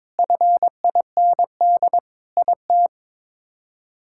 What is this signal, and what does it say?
Morse code, heard as a keyed tone, about 22 wpm, 700 Hz. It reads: FIND IT
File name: practice / 22wpm / Sentences with 3-Word Phrases - 22wpm